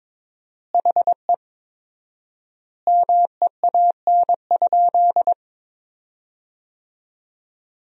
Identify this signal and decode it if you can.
Morse code, heard as a keyed tone, about 22 wpm, 700 Hz. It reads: HE MEAN?
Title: 4x Word-Spacing - Sentences from Top 200 Words - 22wpm